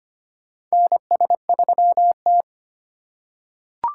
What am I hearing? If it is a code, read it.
Morse code, heard as a keyed tone, about 25 wpm, 700 Hz. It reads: NS3T
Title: USA Call Signs - 3x Character Spacing - 25wpm